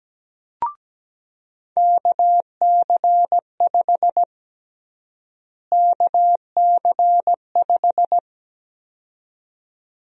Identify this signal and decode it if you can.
Morse code, heard as a keyed tone, about 17 wpm, 700 Hz. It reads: KC5 KC5
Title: USA Call Sign Prefixes - Repeated 2x - 17wpm